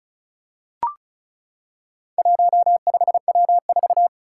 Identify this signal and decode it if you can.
Morse code, heard as a keyed tone, about 36 wpm, 700 Hz. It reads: 15W4
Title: Four Challenging Letter-Number Combinations 35wpm